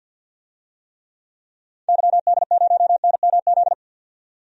Morse code, heard as a keyed tone, about 51 wpm, 700 Hz. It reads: YD0NMZ
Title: DXCC-100 Call Signs - 6 Characters - 50wpm